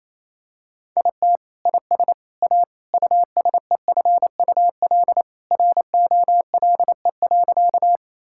Morse code, heard as a keyed tone, about 28 wpm, 700 Hz. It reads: IT IS A USEFUL ROLE.
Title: Sentences from Top 1000 Words - Review Part 6 - 28wpm